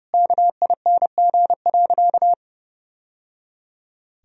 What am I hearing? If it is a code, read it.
Morse code, heard as a keyed tone, about 30 wpm, 700 Hz. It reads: KING.
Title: Sentences from Top 1000 Words - Review Part 4 - 30wpm